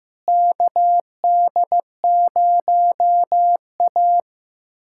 Morse code, heard as a keyed tone, about 15 wpm, 700 Hz. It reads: KD0A
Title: USA Callsign DE Callsign - 4 Characters - 15wpm